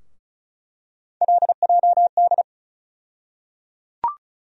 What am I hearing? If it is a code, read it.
Morse code, heard as a keyed tone, about 34 wpm, 700 Hz. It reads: LJD
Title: USA Call Sign Postfixes - 3 Letters 35wpm